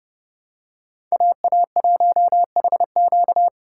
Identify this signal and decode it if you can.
Morse code, heard as a keyed tone, about 30 wpm, 700 Hz. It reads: AA1HQ
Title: USA Call Signs - 5 Characters - 3x Character Spacing - 30wpm